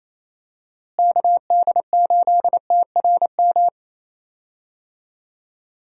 Morse code, heard as a keyed tone, about 28 wpm, 700 Hz. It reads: KD8TRM
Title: USA Call Signs - 6 Characters - Repeated 2x - 28wpm